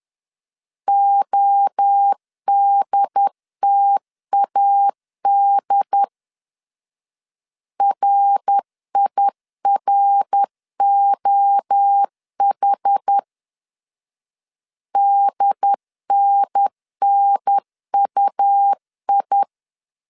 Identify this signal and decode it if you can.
Morse code, heard as a keyed tone, about 10 wpm, 800 Hz. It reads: ODTAD RIROH DNNUI
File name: CW / DET / 1 3 1500C S B 10wpm 3335